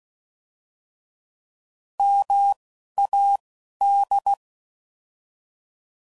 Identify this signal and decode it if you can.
Morse code, heard as a keyed tone, about 16 wpm, spaced as for 11 wpm, 800 Hz. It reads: MAD